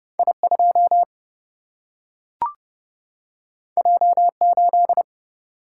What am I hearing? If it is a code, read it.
Morse code, heard as a keyed tone, about 30 wpm, 700 Hz. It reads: I2 J8
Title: Single-Letter Single-Number 30wpm